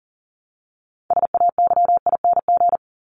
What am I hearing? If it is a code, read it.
Morse code, heard as a keyed tone, about 40 wpm, 700 Hz. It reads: SAYING
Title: ING Suffix - Top 100 Words with Base Words 40wpm